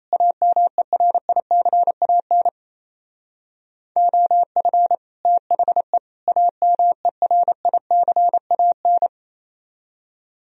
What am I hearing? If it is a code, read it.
Morse code, heard as a keyed tone, about 35 wpm, 700 Hz. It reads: AMERICAN OF THE AMERICAN
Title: Speed-Racing - Sets of 3 Words 28wpm